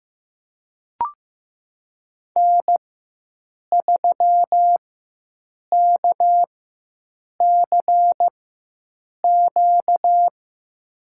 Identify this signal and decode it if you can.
Morse code, heard as a keyed tone, about 16 wpm, spaced as for 7 wpm, 700 Hz. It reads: N3KCQ
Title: USA Call Signs - 5 Characters - 4x Character Spacing - 15wpm